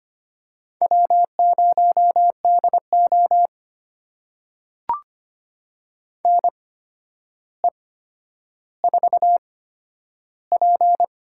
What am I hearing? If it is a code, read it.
Morse code, heard as a keyed tone, about 25 wpm, spaced as for 11 wpm, 700 Hz. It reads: W0DO NE4P
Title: USA Call Signs - 8x Character Spacing - 25wpm